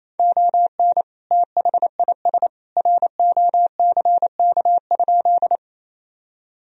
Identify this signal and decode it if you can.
Morse code, heard as a keyed tone, about 28 wpm, 700 Hz. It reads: ON THIS ROCK?
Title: Sentences from Top 1000 Words - Review Part 2 - 28wpm